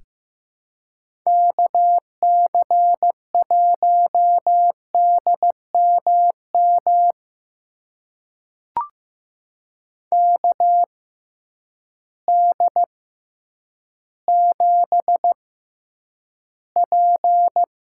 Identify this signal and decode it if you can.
Morse code, heard as a keyed tone, about 15 wpm, spaced as for 9 wpm, 700 Hz. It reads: KC1DMM KD7P
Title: USA Call Signs - 6 Characters - 6x Character Spacing - 15wpm